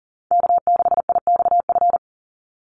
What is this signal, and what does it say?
Morse code, heard as a keyed tone, about 40 wpm, 700 Hz. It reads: K6IXF
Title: USA Call Signs 40wpm